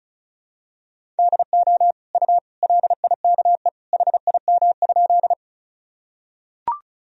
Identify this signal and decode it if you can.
Morse code, heard as a keyed tone, about 34 wpm, 700 Hz. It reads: DO U LIKE HIM?
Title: Sentences from Top 100 Words and Abbreviations 35wpm